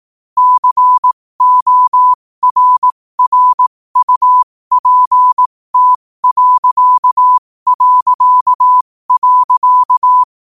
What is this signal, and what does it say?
Morse code, heard as a keyed tone, about 18 wpm, spaced as for 15 wpm, 1000 Hz. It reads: CORRUPT...